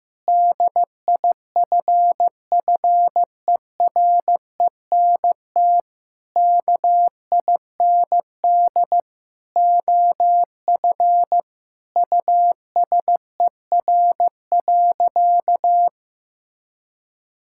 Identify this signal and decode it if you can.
Morse code, heard as a keyed tone, about 15 wpm, 700 Hz. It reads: DIFFERENT KIND OF USER.
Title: Sentences from Top 1000 Words - Review Part 5 - 15wpm